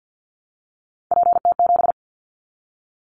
Morse code, heard as a keyed tone, about 50 wpm, 700 Hz. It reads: PT7
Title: Callsign Prefixes for Top 50 DXCC - 50wpm